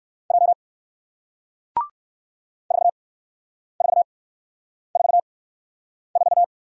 Morse code, heard as a keyed tone, about 58 wpm, spaced as for 9 wpm, 700 Hz. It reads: J 4444